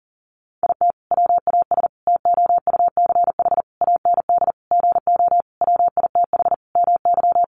Morse code, heard as a keyed tone, about 37 wpm, 700 Hz. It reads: IT WAS TOUCH AND GO WITH MY
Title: Sentences from Top 1200 Words 40wpm